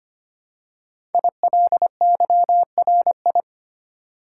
Morse code, heard as a keyed tone, about 25 wpm, 700 Hz. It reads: ILYRI
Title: Code Groups - 5 Letters - 25wpm